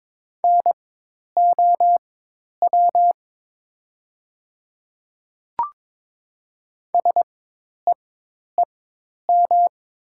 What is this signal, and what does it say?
Morse code, heard as a keyed tone, about 22 wpm, spaced as for 10 wpm, 700 Hz. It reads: NOW SEEM
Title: 4x Character Spacing - Top 500 Words - 22wpm